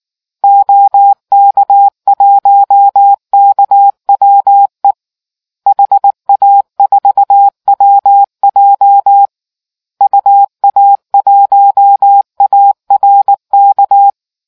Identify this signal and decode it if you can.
Morse code, heard as a keyed tone, about 19 wpm, 800 Hz. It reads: OK1KWE HA4WJ UA1ARK